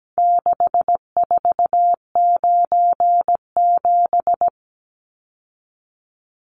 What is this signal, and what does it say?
Morse code, heard as a keyed tone, about 17 wpm, 700 Hz. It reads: 6497
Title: Numbers - 4-Digit 17wpm